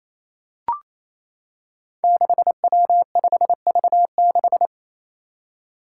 Morse code, heard as a keyed tone, about 27 wpm, 700 Hz. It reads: 6W5V6
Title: Five Challenging Letter-Number Combinations 28wpm